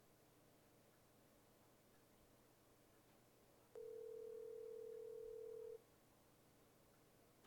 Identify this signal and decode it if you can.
Ringback tone (North America), 440 + 480 Hz.
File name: Cell Phone Dial